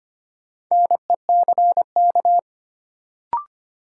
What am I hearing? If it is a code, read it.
Morse code, heard as a keyed tone, about 25 wpm, 700 Hz. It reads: NECK
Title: Sets of 5 Words Encore - New Word Drill - 25wpm